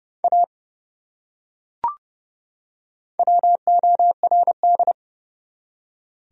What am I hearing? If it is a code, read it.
Morse code, heard as a keyed tone, about 30 wpm, 700 Hz. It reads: A WORD
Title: Top 100 Words 30wpm